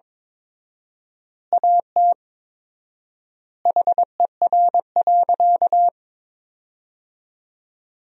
Morse code, heard as a keyed tone, about 22 wpm, 700 Hz. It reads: AT HER.
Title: 4x Word-Spacing - Sentences from Top 200 Words - 22wpm